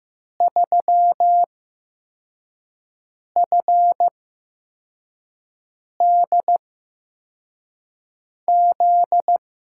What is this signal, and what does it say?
Morse code, heard as a keyed tone, about 15 wpm, spaced as for 4 wpm, 700 Hz. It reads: 3FDZ
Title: USA Call Signs - 6 Characters - 8x Character Spacing - 15wpm